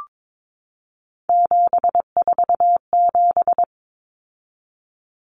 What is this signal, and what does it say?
Morse code, heard as a keyed tone, about 22 wpm, 700 Hz. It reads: 747
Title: Common Cut Numbers - 3-Digit 22wpm